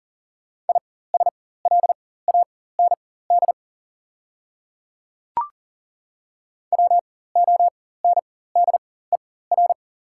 Morse code, heard as a keyed tone, about 40 wpm, spaced as for 19 wpm, 690 Hz. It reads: ISLAND WONDER
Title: 4x Character Spacing - Top 500 Words - 40wpm